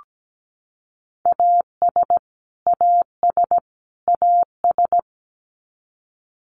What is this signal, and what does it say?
Morse code, heard as a keyed tone, about 17 wpm, 700 Hz. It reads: AS AS AS
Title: Top 500 Words Repeated 17wpm